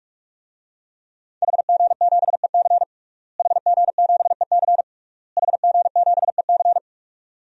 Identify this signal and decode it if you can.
Morse code, heard as a keyed tone, about 46 wpm, 700 Hz. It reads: HG7EC HG7EC HG7EC
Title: DXCC-100 Call Signs - 5 Characters - Repeated 3x - 45wpm